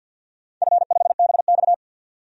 Morse code, heard as a keyed tone, about 48 wpm, 700 Hz. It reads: F5BX
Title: DXCC-100 Call Signs - 4 Characters - 50wpm